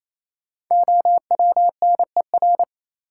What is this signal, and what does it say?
Morse code, heard as a keyed tone, about 28 wpm, 700 Hz. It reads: OWNER
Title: Sentences from Top 1600 Words - New Word Drill Alt - 28wpm